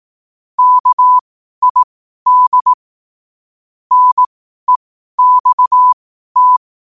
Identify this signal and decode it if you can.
Morse code, heard as a keyed tone, about 18 wpm, spaced as for 12 wpm, 1000 Hz. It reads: KID NEXT
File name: qotdfiles / 12wpm